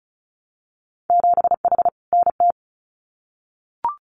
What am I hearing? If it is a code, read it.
Morse code, heard as a keyed tone, about 34 wpm, 700 Hz. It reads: 7H NT
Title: Winter Field Day Exchanges Encore 35wpm